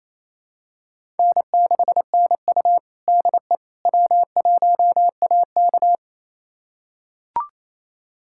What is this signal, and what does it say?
Morse code, heard as a keyed tone, about 28 wpm, 700 Hz. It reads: N6NU DE W1AK
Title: USA Callsign DE Callsign - 4 Characters - 28wpm